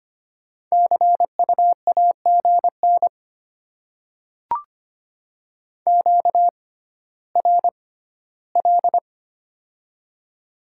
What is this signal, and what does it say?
Morse code, heard as a keyed tone, about 25 wpm, spaced as for 14 wpm, 700 Hz. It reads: CU AGN QRL
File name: Top 20 QSO Elements - 6x Character Spacing - 25wpm